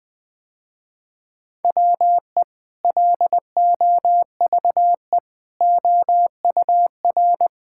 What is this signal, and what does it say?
Morse code, heard as a keyed tone, about 20 wpm, 700 Hz. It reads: WE LOVE OUR
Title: Sentences with 2-Word Phrases - Spoken Beginning - 20wpm